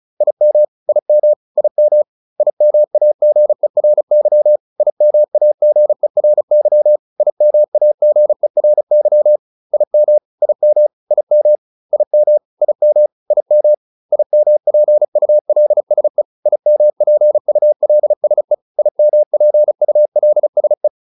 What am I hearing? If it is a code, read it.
Morse code, heard as a keyed tone, about 34 wpm, 600 Hz. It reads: IM IM IM IMAGERY IMAGERY IMAGERY IM IM IM IM IM IM IMPULSE IMPULSE IMPULSE